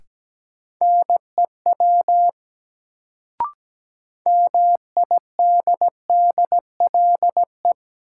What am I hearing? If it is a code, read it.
Morse code, heard as a keyed tone, about 17 wpm, 700 Hz. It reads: NEW MIDDLE
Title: Sets of 2 Words - New Word Drill - 17wpm